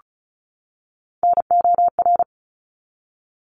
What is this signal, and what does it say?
Morse code, heard as a keyed tone, about 34 wpm, 700 Hz. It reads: NOR
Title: Top 1251-1500 Words 35wpm